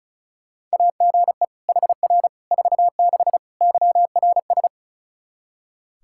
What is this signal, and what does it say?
Morse code, heard as a keyed tone, about 34 wpm, 700 Hz. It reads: AGE HR 46 YRS
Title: Practice QSO - Exchange 3 - More Info - 35wpm